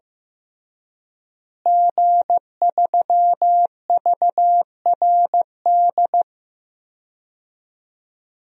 Morse code, heard as a keyed tone, about 15 wpm, 700 Hz. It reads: G3VRD
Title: DXCC-100 Call Signs - 5 Characters - Repeated 2x - 15wpm